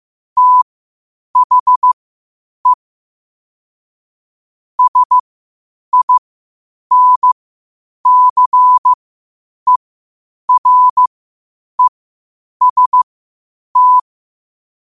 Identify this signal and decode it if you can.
Morse code, heard as a keyed tone, about 15 wpm, spaced as for 8 wpm, 1000 Hz. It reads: THE SINCEREST
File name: qotdfiles / 8wpm